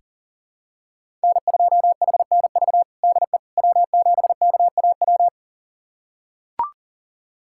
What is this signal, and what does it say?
Morse code, heard as a keyed tone, about 40 wpm, 700 Hz. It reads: N2HNV DE W7KAW
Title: USA Callsign DE Callsign - 5 Characters - 40wpm